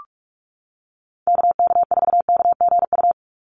Morse code, heard as a keyed tone, about 44 wpm, 700 Hz. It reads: KK4KGU
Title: USA Call Signs 45wpm